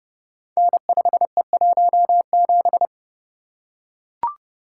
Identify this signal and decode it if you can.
Morse code, heard as a keyed tone, about 30 wpm, 700 Hz. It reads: N5E17
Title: Code Groups - 5 Letter:Numbers - 30wpm